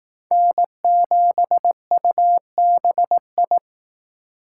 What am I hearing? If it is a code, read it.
Morse code, heard as a keyed tone, about 18 wpm, 700 Hz. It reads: N7UBI
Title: Speed Racing - USA 1x3 Call Signs 15wpm